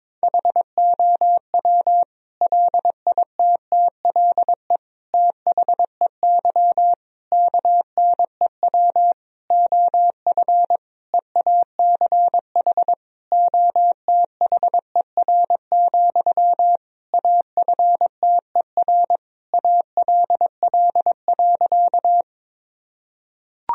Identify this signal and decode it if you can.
Morse code, heard as a keyed tone, about 22 wpm, 700 Hz. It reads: HOW LITTLE THEY KNEW OF EACH OTHER, AFTER ALL.
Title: Sentences from Top 1000 Words - Review Part 1 - 22wpm